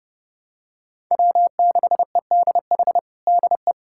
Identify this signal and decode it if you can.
Morse code, heard as a keyed tone, about 30 wpm, 700 Hz. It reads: W6EDH DE
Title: USA Callsign DE Callsign 30wpm